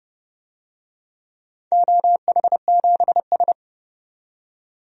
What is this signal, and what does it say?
Morse code, heard as a keyed tone, about 32 wpm, 700 Hz. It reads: OH7S
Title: DXCC-100 Call Signs - 4 Characters - 30wpm